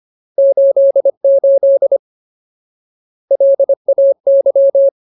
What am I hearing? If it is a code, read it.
Morse code, heard as a keyed tone, about 25 wpm, 550 Hz. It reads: 88 LAY
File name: Ultimate Word List / 25WPM Ultimate